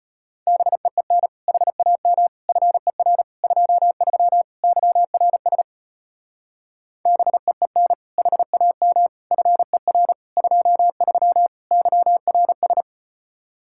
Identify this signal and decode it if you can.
Morse code, heard as a keyed tone, about 34 wpm, 700 Hz. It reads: BEEN HAM FER 23 YRS BEEN HAM FER 23 YRS